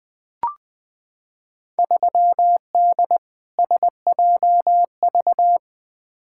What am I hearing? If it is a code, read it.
Morse code, heard as a keyed tone, about 20 wpm, 700 Hz. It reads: 3D SJV
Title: Field Day Exchanges - 20wpm